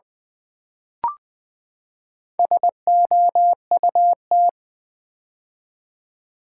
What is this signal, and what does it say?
Morse code, heard as a keyed tone, about 20 wpm, 700 Hz. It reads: SOUT
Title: Build A Word - 4 Letters At A Time - 8-Letter Words - 20wpm